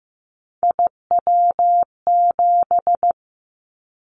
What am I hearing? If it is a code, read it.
Morse code, heard as a keyed tone, about 15 wpm, 700 Hz. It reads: IW7
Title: Callsign Prefixes for Top 25 DXCC - 15wpm